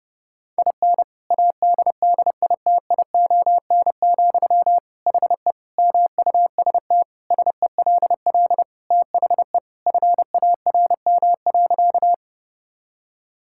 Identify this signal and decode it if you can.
Morse code, heard as a keyed tone, about 31 wpm, 700 Hz. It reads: IN ADDITION, HE MUST SELL THE FARM.
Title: Sentences from Top 1000 Words - Review Part 6 - 30wpm